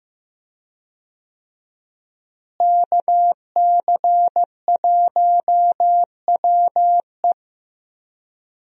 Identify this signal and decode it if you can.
Morse code, heard as a keyed tone, about 16 wpm, 700 Hz. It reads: KC1WE
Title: USA Call Signs - 5 Characters - Repeated 2x - 15wpm